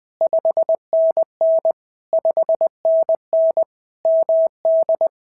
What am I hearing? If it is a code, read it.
Morse code, heard as a keyed tone, about 19 wpm, 650 Hz. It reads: 5NN 5NN MD
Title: POTA QSO Practice - 20wpm